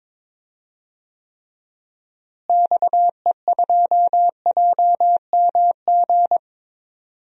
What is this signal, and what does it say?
Morse code, heard as a keyed tone, about 21 wpm, 700 Hz. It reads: XE2JMG
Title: DXCC-100 Call Signs - 6 Characters - Repeated 2x - 22wpm